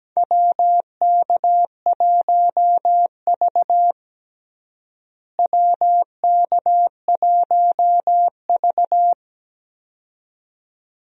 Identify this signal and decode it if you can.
Morse code, heard as a keyed tone, about 17 wpm, 700 Hz. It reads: WK1V WK1V
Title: USA Call Signs - 4 Characters - Repeated 2x - 17wpm